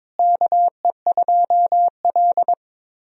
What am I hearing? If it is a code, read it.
Morse code, heard as a keyed tone, about 22 wpm, 700 Hz. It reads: KE2L
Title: USA Call Signs - 4 Characters - Repeated 3x - 22wpm